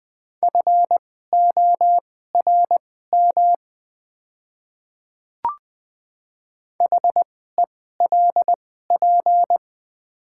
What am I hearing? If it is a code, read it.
Morse code, heard as a keyed tone, about 20 wpm, spaced as for 14 wpm, 690 Hz. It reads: FORM HELP
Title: 2x Character Spacing - Top 500 Words - 20wpm